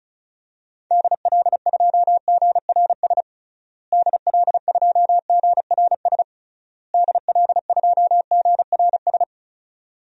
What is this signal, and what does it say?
Morse code, heard as a keyed tone, about 34 wpm, 700 Hz. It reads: DL2GRS DL2GRS DL2GRS
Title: DXCC-100 Call Signs - 6 Characters - Repeated 3x - 35wpm